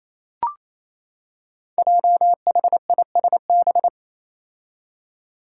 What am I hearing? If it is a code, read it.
Morse code, heard as a keyed tone, about 28 wpm, 700 Hz. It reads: JHISB
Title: Five Challenging Letter-Number Combinations 28wpm